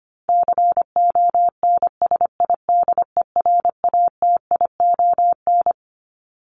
Morse code, heard as a keyed tone, about 25 wpm, 700 Hz. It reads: CONSIDERATION
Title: Top 1251-1500 Words 25wpm